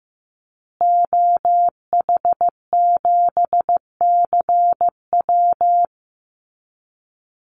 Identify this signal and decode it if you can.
Morse code, heard as a keyed tone, about 15 wpm, 700 Hz. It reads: OH7CW
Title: Call Signs Repeated 2x - 15wpm